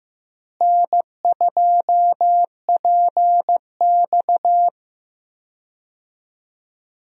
Morse code, heard as a keyed tone, about 15 wpm, 700 Hz. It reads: N2PX
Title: USA Call Signs - 4 Characters - Repeated 3x - 15wpm